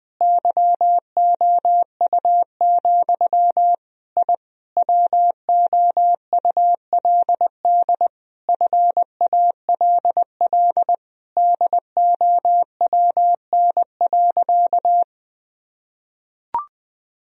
Morse code, heard as a keyed tone, about 20 wpm, 700 Hz. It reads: YOU, I WOULD FALL DOWN.
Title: Sentences from Top 1000 Words - Review Part 2 - 20wpm